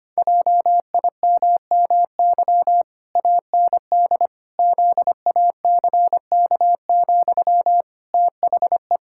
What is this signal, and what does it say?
Morse code, heard as a keyed tone, about 25 wpm, 700 Hz. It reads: JIMMY AND ZACK, THE
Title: Pangram Challenge v2 - 25wpm